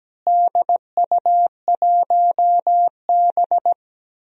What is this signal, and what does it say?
Morse code, heard as a keyed tone, about 17 wpm, 700 Hz. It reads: DU1B
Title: DXCC-100 Call Signs - 4 Characters - Repeated 3x - 17wpm